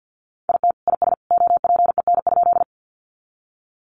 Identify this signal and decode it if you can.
Morse code, heard as a keyed tone, about 50 wpm, 700 Hz. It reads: IT IS OPEN?
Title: Sentences from Top 300 Words 50wpm